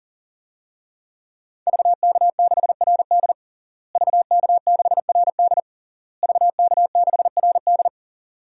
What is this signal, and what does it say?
Morse code, heard as a keyed tone, about 40 wpm, 700 Hz. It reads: VK6RD VK6RD VK6RD
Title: DXCC-100 Call Signs - 5 Characters - Repeated 3x - 40wpm